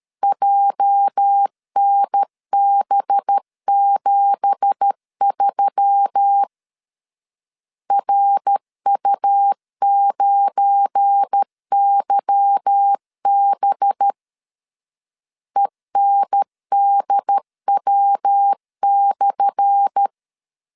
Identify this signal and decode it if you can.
Morse code, heard as a keyed tone, about 13 wpm, 800 Hz. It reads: JNB73 RU9YB ENDW/
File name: LEZIONI / 10 SIGNS / 10A SLASH 12wpm 2314